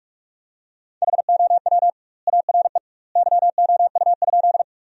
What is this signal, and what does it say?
Morse code, heard as a keyed tone, about 46 wpm, 700 Hz. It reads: HOW ARE YOU?
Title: 2x Word-Spacing - Sentences from Top 100 Words - 45wpm